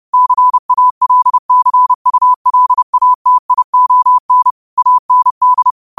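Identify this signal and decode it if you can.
Morse code, heard as a keyed tone, about 30 wpm, 1000 Hz. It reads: CALCULATION AND
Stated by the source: Morse code quote of the day sent at 30 wpm.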